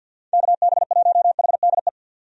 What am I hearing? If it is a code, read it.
Morse code, heard as a keyed tone, about 50 wpm, 700 Hz. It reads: KB1HDE
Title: USA Call Signs - 6 Characters - 50wpm